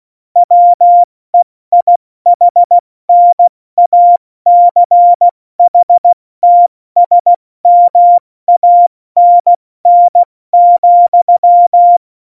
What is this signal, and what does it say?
Morse code, heard as a keyed tone, about 16 wpm, 700 Hz. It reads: WEIHNACHTSMANN,